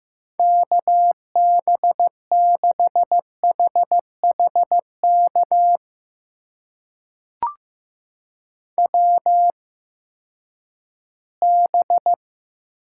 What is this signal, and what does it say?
Morse code, heard as a keyed tone, about 15 wpm, spaced as for 10 wpm, 700 Hz. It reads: KB6HHK WB
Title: USA Call Signs - 6 Characters - 8x Character Spacing - 15wpm